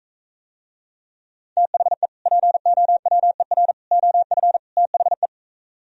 Morse code, heard as a keyed tone, about 45 wpm, 700 Hz. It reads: THE POWER OF THE